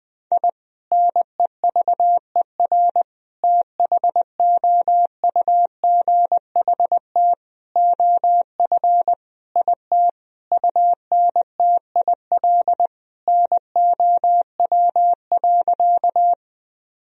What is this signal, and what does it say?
Morse code, heard as a keyed tone, about 19 wpm, 700 Hz. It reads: I NEVER THOUGHT OF IT UNTIL NOW.
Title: Sentences from Top 1000 Words - Review Part 1 - 20wpm